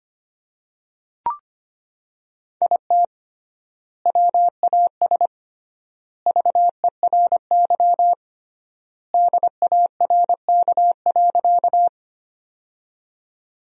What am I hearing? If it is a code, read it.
Morse code, heard as a keyed tone, about 25 wpm, 700 Hz. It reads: IT WAS VERY DARK.
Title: 3x Word-Spacing - Sentences from Top 500 Words - 25wpm